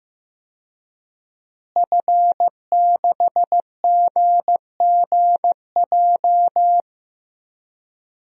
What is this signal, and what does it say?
Morse code, heard as a keyed tone, about 15 wpm, 700 Hz. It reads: F6GGJ
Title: DXCC-100 Call Signs - 5 Characters - Repeated 2x - 15wpm